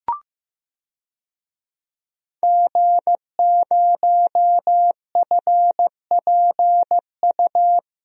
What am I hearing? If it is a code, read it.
Morse code, heard as a keyed tone, about 15 wpm, 700 Hz. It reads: G0FPU
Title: DXCC-100 Call Signs - 5 Characters - 15wpm